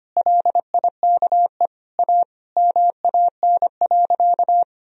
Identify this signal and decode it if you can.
Morse code, heard as a keyed tone, about 25 wpm, 700 Hz. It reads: LIKE A MAN.
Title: Sentences from Top 300 Words and Abbreviations 25wpm